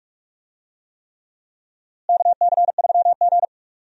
Morse code, heard as a keyed tone, about 46 wpm, 700 Hz. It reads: KC3G
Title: USA Call Signs - 4 Characters - Repeated 3x - 45wpm